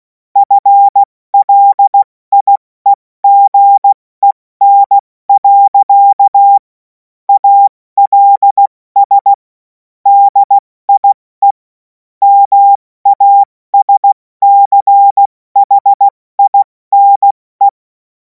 Morse code, heard as a keyed tone, about 16 wpm, spaced as for 14 wpm, 800 Hz. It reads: FLIEGEN. ALS DIE MASCHINE
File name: Morsekurs / Witze / Witz 31. März 2026 70 F80